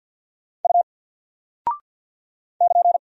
Morse code, heard as a keyed tone, about 50 wpm, 700 Hz. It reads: U <KN>
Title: Single Character (Letter:Number:Punctuation:Prosign) Alt 50wpm